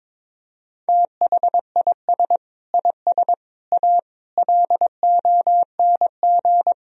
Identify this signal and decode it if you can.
Morse code, heard as a keyed tone, about 22 wpm, 700 Hz. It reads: THIS IS A LONG
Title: Sentences from Top 1000 Words - Review Part 2 - 22wpm